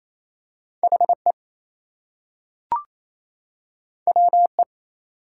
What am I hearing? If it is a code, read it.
Morse code, heard as a keyed tone, about 28 wpm, 700 Hz. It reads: HE WE
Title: Two Letter Words - 28wpm